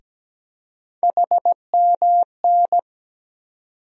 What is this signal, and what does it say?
Morse code, heard as a keyed tone, about 17 wpm, 700 Hz. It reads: HMN
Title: USA Call Sign Postfixes - 3 Letters 17wpm